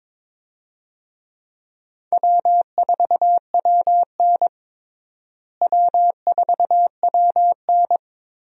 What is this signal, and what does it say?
Morse code, heard as a keyed tone, about 22 wpm, 700 Hz. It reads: W4WN W4WN
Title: USA Call Signs - 4 Characters - Repeated 2x - 22wpm